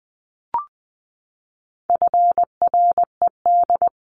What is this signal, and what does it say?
Morse code, heard as a keyed tone, about 20 wpm, 700 Hz. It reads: FRED
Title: CWT Exchange 20wpm